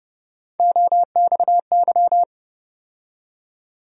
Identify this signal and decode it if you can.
Morse code, heard as a keyed tone, about 31 wpm, 700 Hz. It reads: OXY
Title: USA Call Sign Postfixes - 3 Letters 30wpm